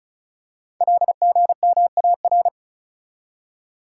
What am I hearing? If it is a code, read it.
Morse code, heard as a keyed tone, about 36 wpm, 700 Hz. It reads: LGMAR
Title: Code Groups - 5 Letters - 35wpm